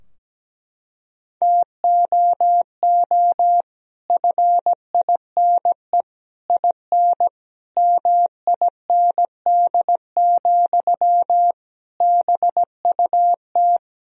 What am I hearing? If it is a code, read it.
Morse code, heard as a keyed tone, about 17 wpm, 700 Hz. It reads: TOO FINE IN MIND, BUT
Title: Sentences from Top 1000 Words - Review Part 4 - 17wpm